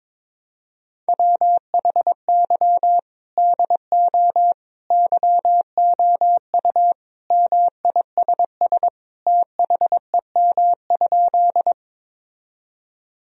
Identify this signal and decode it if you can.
Morse code, heard as a keyed tone, about 22 wpm, 700 Hz. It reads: WHY DO YOU MISS THEM?
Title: Sentences from Top 1000 Words - Review Part 3 - 22wpm